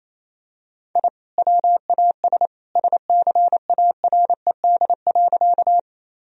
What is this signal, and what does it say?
Morse code, heard as a keyed tone, about 28 wpm, 700 Hz. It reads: I WAS SCARED.
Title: Sentences from Top 1000 Words - Review Part 4 - 28wpm